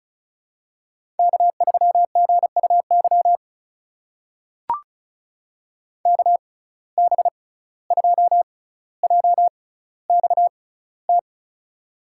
Morse code, heard as a keyed tone, about 33 wpm, spaced as for 17 wpm, 700 Hz. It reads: K3GUY KB2JXT
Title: USA Call Signs - 6x Character Spacing - 35wpm